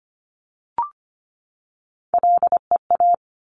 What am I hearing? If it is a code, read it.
Morse code, heard as a keyed tone, about 25 wpm, 700 Hz. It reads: LEA
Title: Top Three Letter Combinations in Words 25wpm